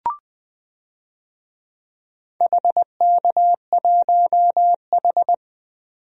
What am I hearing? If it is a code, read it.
Morse code, heard as a keyed tone, about 20 wpm, 700 Hz. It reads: HK1H
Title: DXCC-100 Call Signs - 4 Characters - 20wpm